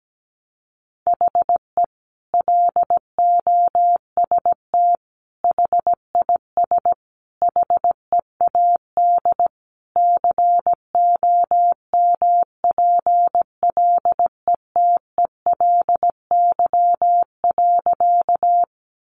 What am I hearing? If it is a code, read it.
Morse code, heard as a keyed tone, about 17 wpm, 700 Hz. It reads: HE LOST HIS HEAD COMPLETELY.
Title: Sentences from Top 1400 Words 17wpm